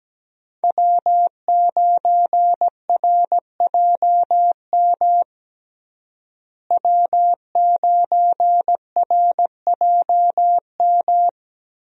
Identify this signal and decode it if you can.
Morse code, heard as a keyed tone, about 17 wpm, 700 Hz. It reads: W9RJM W9RJM
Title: USA Call Signs - 5 Characters - Repeated 2x - 17wpm